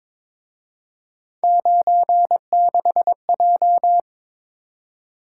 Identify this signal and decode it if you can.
Morse code, heard as a keyed tone, about 26 wpm, 700 Hz. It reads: 96J
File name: Speed-Racing - Three Letter-Number Combinations - 22wpm